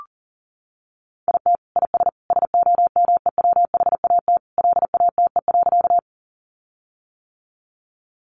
Morse code, heard as a keyed tone, about 37 wpm, 700 Hz. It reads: IT IS SOMEWHAT LATE.
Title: Sentences from Top 1300 Words 40wpm